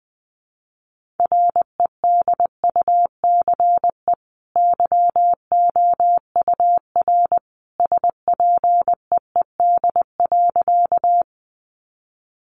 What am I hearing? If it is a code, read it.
Morse code, heard as a keyed tone, about 20 wpm, 700 Hz. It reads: REDUCE YOUR SPEED.
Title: Sentences from Top 1100 Words 20wpm